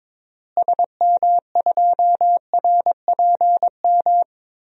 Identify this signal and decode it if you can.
Morse code, heard as a keyed tone, about 22 wpm, 700 Hz. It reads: SM2RPM
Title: DXCC-100 Call Signs - 6 Characters - Repeated 3x - 22wpm